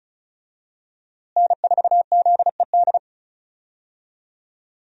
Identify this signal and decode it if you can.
Morse code, heard as a keyed tone, about 34 wpm, 700 Hz. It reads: N4ZED
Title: USA Call Signs - 5 Characters - Repeated 3x - 35wpm